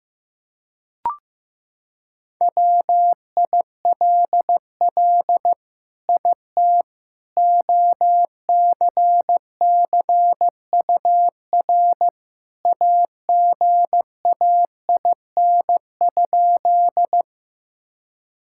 Morse code, heard as a keyed tone, about 15 wpm, 700 Hz. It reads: WILL IT OCCUR AGAIN?
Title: Sentences from Top 1000 Words - Review Part 6 - 15wpm